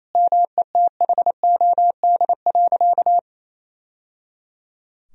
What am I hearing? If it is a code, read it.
Morse code, heard as a keyed tone, about 28 wpm, 700 Hz. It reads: METHOD.
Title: Sentences from Top 1000 Words - Review Part 3 - 28wpm